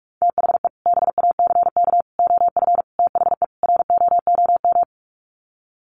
Morse code, heard as a keyed tone, about 44 wpm, 700 Hz. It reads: THE BACK OF THE ROOM
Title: Sets of 5 Words Encore 45wpm